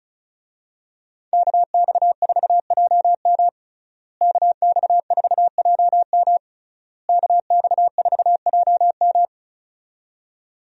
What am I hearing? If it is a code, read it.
Morse code, heard as a keyed tone, about 34 wpm, 700 Hz. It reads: KX4JM KX4JM KX4JM
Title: USA Call Signs - 5 Characters - Repeated 3x - 35wpm